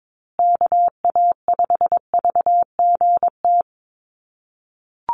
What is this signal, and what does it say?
Morse code, heard as a keyed tone, about 22 wpm, 700 Hz. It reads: KA5VGT
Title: USA Call Signs Repeated 3x 22wpm